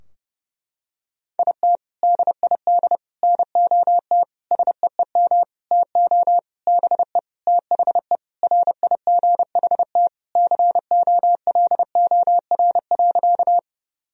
Morse code, heard as a keyed tone, about 30 wpm, 700 Hz. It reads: IT DID NOT SEEM TO BE THE RIGHT COLOR.
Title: Sentences from Top 1000 Words - Review Part 2 - 30wpm